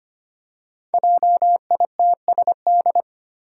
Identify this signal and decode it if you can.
Morse code, heard as a keyed tone, about 25 wpm, 700 Hz. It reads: JITSD
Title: Five Challenging Letter-Number Combinations 25wpm